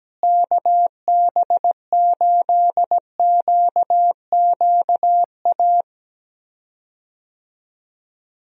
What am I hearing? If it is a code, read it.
Morse code, heard as a keyed tone, about 17 wpm, 700 Hz. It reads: KB8QQA
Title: USA Call Signs - 6 Characters - Repeated 2x - 17wpm